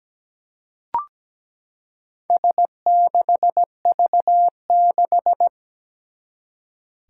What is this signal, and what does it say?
Morse code, heard as a keyed tone, about 17 wpm, 700 Hz. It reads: S6V6
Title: Four Challenging Letter-Number Combinations 17wpm